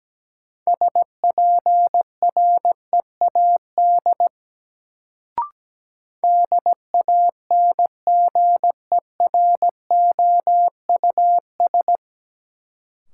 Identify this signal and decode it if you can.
Morse code, heard as a keyed tone, about 17 wpm, 700 Hz. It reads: SPREAD DANGEROUS
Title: Sentences from Top 1200 Words - New Word Drill - 17wpm